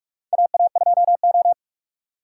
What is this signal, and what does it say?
Morse code, heard as a keyed tone, about 46 wpm, 700 Hz. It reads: AA2O
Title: USA Call Signs - 4 Characters - 6x Character Spacing - 45wpm